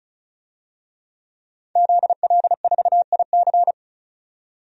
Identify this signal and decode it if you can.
Morse code, heard as a keyed tone, about 36 wpm, 700 Hz. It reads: ZL4IC
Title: DXCC-100 Call Signs - 5 Characters - 35wpm